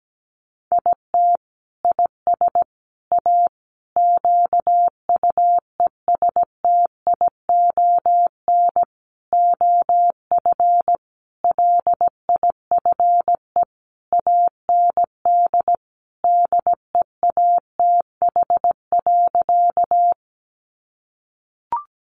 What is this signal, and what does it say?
Morse code, heard as a keyed tone, about 17 wpm, 700 Hz. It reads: IT IS A QUESTION OF LIFE AND DEATH.
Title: Sentences from Top 1100 Words 17wpm